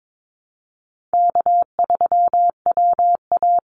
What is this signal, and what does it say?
Morse code, heard as a keyed tone, about 22 wpm, 700 Hz. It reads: K3WA
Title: Easy Call Signs - 22wpm